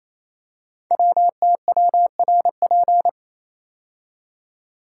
Morse code, heard as a keyed tone, about 27 wpm, 700 Hz. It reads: WTWRP
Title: Code Groups - 5 Letters - 28wpm